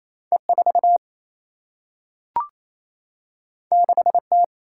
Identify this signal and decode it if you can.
Morse code, heard as a keyed tone, about 28 wpm, 700 Hz. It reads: E 4 6 T
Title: Single-Letter Single-Number 28wpm